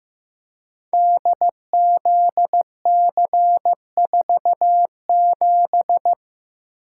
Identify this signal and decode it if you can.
Morse code, heard as a keyed tone, about 16 wpm, 700 Hz. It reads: DZC47
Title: Code Groups - 5 Letter:Numbers - 15wpm